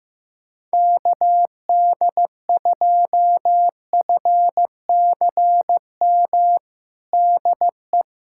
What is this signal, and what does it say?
Morse code, heard as a keyed tone, about 16 wpm, 700 Hz. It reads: KD2FCM DE
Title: USA Callsign DE Callsign - 6 Characters - 15wpm